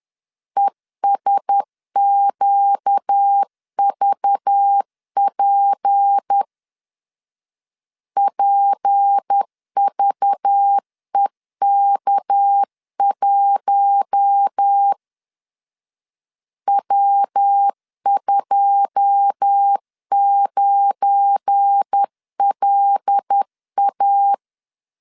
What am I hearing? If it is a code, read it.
Morse code, heard as a keyed tone, about 10 wpm, 800 Hz. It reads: ESQVP PVEK1 W29LA
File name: LEZIONI / 9 2 10wpm 2736